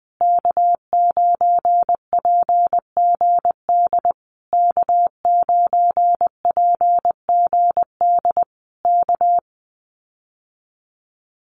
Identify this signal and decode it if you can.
Morse code, heard as a keyed tone, about 20 wpm, 700 Hz. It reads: K9PGD K9PGD K
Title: Common QSO Phrases Encore 20wpm